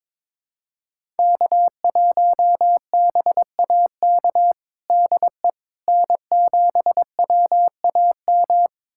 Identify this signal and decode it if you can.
Morse code, heard as a keyed tone, about 22 wpm, 700 Hz. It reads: K1BAK DE N7WAM
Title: USA Callsign DE Callsign - 5 Characters - 22wpm